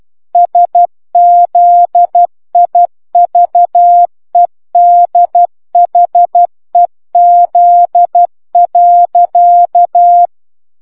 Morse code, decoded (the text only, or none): SZIVEDHEZ.